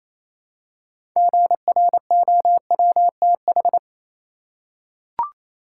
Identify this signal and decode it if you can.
Morse code, heard as a keyed tone, about 28 wpm, 700 Hz. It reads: GROWTH
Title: Six Letter Words - 28wpm